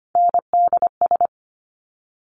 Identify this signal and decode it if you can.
Morse code, heard as a keyed tone, about 25 wpm, 700 Hz. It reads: NDS
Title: Top Three Letter Combinations in Words 25wpm